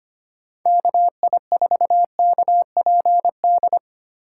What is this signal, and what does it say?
Morse code, heard as a keyed tone, about 25 wpm, 700 Hz. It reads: KI4KPD
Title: USA Call Signs - 6 Characters - 3x Character Spacing - 25wpm